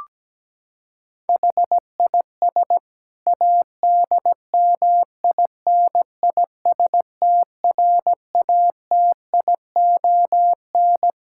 Morse code, heard as a keyed tone, about 17 wpm, 700 Hz. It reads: HIS ADMINISTRATION
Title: Sentences from Top 1600 Words 17wpm